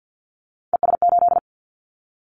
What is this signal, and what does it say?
Morse code, heard as a keyed tone, about 50 wpm, 700 Hz. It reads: ES8
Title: Callsign Prefixes for Top 50 DXCC - 50wpm